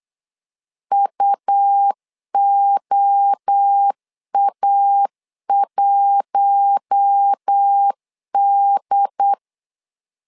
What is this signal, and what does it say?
Morse code, heard as a keyed tone, about 8 wpm, 800 Hz. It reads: UOA1D
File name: LEZIONI / 4 Q1S3 8wpm 3037